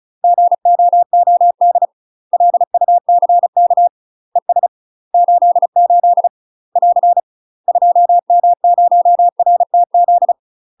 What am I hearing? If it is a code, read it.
Morse code, heard as a keyed tone, about 34 wpm, 700 Hz. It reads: GOOD LUCK ES 88 <AR> 2M0RTZ
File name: UK MASTER QSOs TRAINING / 35WPM / UK MASTER QSOs 0001